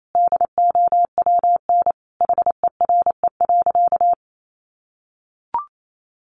Morse code, heard as a keyed tone, about 28 wpm, 700 Hz. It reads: DOWN HERE.
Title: Sentences from Top 1100 Words 28wpm